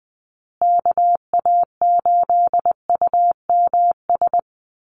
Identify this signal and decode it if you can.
Morse code, heard as a keyed tone, about 20 wpm, 700 Hz. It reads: KA8UMS
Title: USA Call Signs Repeated 2x 20wpm